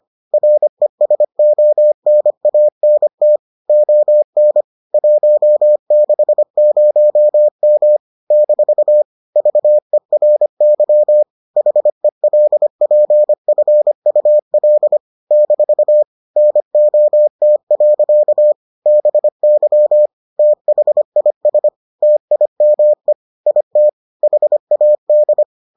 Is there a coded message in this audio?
Yes: Morse code from 0.3 to 25.4 s, heard as a keyed tone, about 25 wpm, 600 Hz. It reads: RESONANT ON 160M - VERY HELPFUL - NOT. BY THIS TIME IT HAD